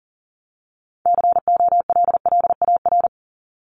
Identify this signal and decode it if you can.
Morse code, heard as a keyed tone, about 37 wpm, 700 Hz. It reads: COLLAR
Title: Top 1751-2000 Words 40wpm